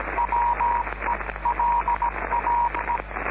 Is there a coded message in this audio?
Yes: Morse code from 0.2 to 3.0 s, heard as a keyed tone, about 17 wpm, 950 Hz, with radio noise. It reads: WELAE